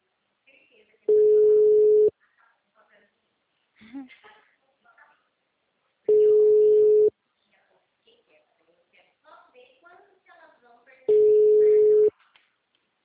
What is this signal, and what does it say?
Ringback tone (Europe, 425 Hz), with speech.